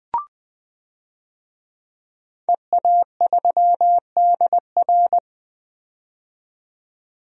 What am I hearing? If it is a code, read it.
Morse code, heard as a keyed tone, about 20 wpm, 700 Hz. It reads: EA3DR
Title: DXCC-100 Call Signs - 5 Characters - 20wpm